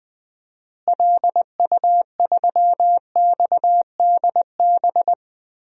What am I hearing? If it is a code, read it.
Morse code, heard as a keyed tone, about 20 wpm, 700 Hz. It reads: LU3XDB
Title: DXCC-100 Call Signs - 6 Characters - Repeated 3x - 20wpm